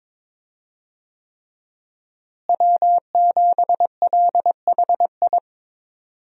Morse code, heard as a keyed tone, about 22 wpm, 700 Hz. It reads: W7LHI
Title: USA Call Signs - 5 Characters - Repeated 3x - 22wpm